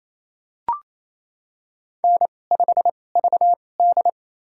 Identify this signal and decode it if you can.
Morse code, heard as a keyed tone, about 28 wpm, spaced as for 20 wpm, 700 Hz. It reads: N5VD
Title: USA Call Signs - 4 Characters - 2x Character Spacing - 28wpm